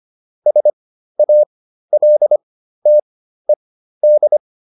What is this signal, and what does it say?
Morse code, heard as a keyed tone, about 25 wpm, spaced as for 13 wpm, 600 Hz. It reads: SALTED